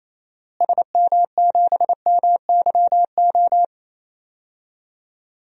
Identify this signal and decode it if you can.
Morse code, heard as a keyed tone, about 28 wpm, 700 Hz. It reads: SM7MYO
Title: DXCC-100 Call Signs - 6 Characters - Repeated 2x - 28wpm